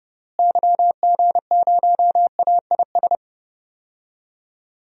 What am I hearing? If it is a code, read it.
Morse code, heard as a keyed tone, about 30 wpm, 700 Hz. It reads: YG0AIS
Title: DXCC-100 Call Signs - 6 Characters - Repeated 3x - 30wpm